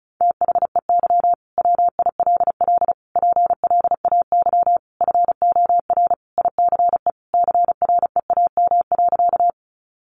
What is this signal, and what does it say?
Morse code, heard as a keyed tone, about 34 wpm, 700 Hz. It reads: THEY WILL PLAY FOR ICE CREAM.
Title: Sentences from Top 1500 Words 35wpm